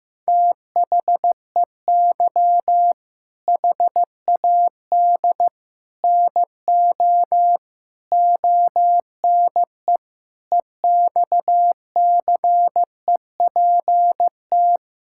Morse code, heard as a keyed tone, about 15 wpm, 700 Hz. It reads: THEY HAD NO ONE EXCEPT
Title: Sentences from Top 1600 Words 15wpm